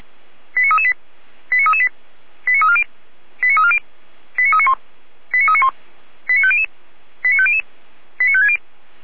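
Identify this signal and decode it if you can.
Five-tone selective call (ZVEI), heard as quick runs of five tones, tones of about 80 ms. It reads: 89338 89338 89340 89340 89391 89391 89500 89500 89560
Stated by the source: ZVEI five-tone selective call received by SDR, from Feuerwehr Orlamünde.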